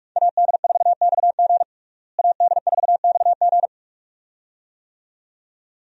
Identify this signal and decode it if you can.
Morse code, heard as a keyed tone, about 41 wpm, 700 Hz. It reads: AD4XG AD4XG
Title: USA Call Signs - 5 Characters - Repeated 2x - 45wpm